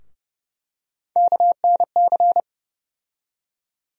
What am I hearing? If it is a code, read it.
Morse code, heard as a keyed tone, about 30 wpm, 700 Hz. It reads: KNC
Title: USA Call Sign Postfixes - 3 Letters 30wpm